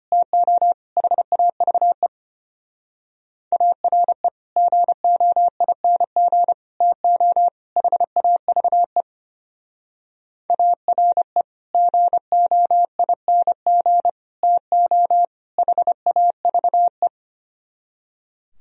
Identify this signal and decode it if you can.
Morse code, heard as a keyed tone, about 32 wpm, 700 Hz. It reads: TO HAVE ARE GOING TO HAVE ARE GOING TO HAVE